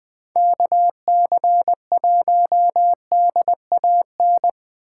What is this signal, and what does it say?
Morse code, heard as a keyed tone, about 21 wpm, 700 Hz. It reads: KC1DAN
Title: Speed Racing - USA Call Signs 17wpm